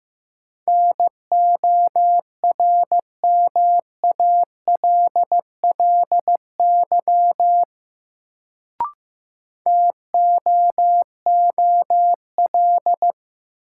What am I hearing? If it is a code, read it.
Morse code, heard as a keyed tone, about 15 wpm, 700 Hz. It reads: NORMALLY TOOL
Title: Sentences from Top 1000 Words - Review Part 6 - New Word Drill - 15wpm